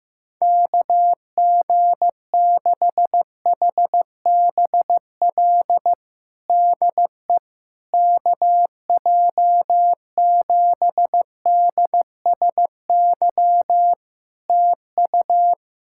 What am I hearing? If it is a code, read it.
Morse code, heard as a keyed tone, about 15 wpm, 700 Hz. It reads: KG6HBL DE KJ7DSY TU
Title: K1USN SST QSO Practice 15wpm